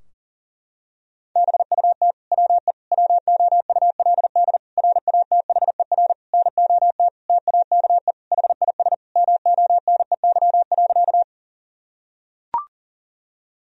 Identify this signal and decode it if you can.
Morse code, heard as a keyed tone, about 40 wpm, 700 Hz. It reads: BUT WE WOULD RATHER NOT TAKE HIS MONEY.
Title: Sentences from Top 1000 Words - Review Part 4 - 40wpm